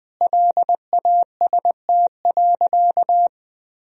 Morse code, heard as a keyed tone, about 20 wpm, 700 Hz. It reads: LAST.
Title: Sentences from Top 300 Words and Abbreviations 20wpm